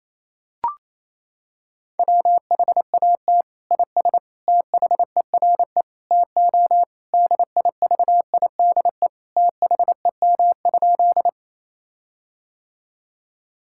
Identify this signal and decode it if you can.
Morse code, heard as a keyed tone, about 28 wpm, 700 Hz. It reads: WHAT IS THERE TO DIVIDE THEM?
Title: Sentences from Top 1600 Words 28wpm